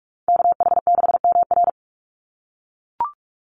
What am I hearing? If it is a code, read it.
Morse code, heard as a keyed tone, about 44 wpm, 700 Hz. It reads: KH6MR
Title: USA Call Signs 45wpm